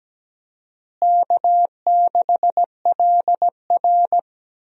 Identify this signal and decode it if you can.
Morse code, heard as a keyed tone, about 17 wpm, 700 Hz. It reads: K6LR
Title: USA Call Signs - 4 Characters - Repeated 3x - 17wpm